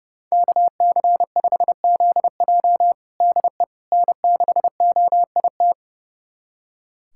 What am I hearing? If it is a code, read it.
Morse code, heard as a keyed tone, about 30 wpm, 700 Hz. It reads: KC5ZJ DE N6OIT
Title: USA Callsign DE Callsign - 5 Characters - 30wpm